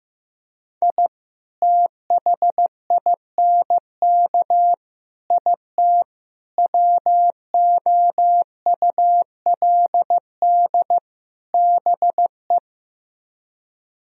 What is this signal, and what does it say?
Morse code, heard as a keyed tone, about 15 wpm, 700 Hz. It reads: I THINK IT WOULD BE
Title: Speed-Racing - Sets of 5 Words 15wpm